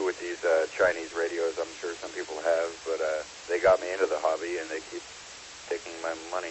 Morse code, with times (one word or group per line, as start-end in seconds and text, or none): none